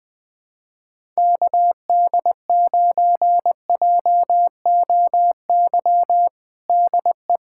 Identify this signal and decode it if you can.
Morse code, heard as a keyed tone, about 20 wpm, 700 Hz. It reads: KD9JOY DE
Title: USA Callsign DE Callsign - 6 Characters - 20wpm